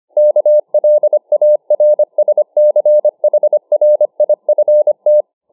Morse code, heard as a keyed tone, about 25 wpm, 600 Hz. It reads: KLARSCHRIFT